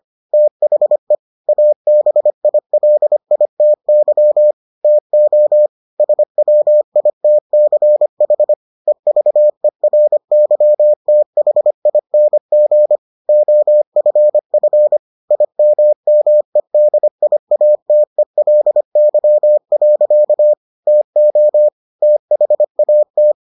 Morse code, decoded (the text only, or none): THE ABILITY TO SWITCH EVERYTHING OFF IMMEDIATELY. TO THAT